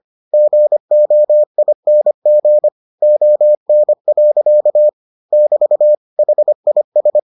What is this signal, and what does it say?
Morse code, heard as a keyed tone, about 25 wpm, 600 Hz. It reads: GOING ON. <BT> HIS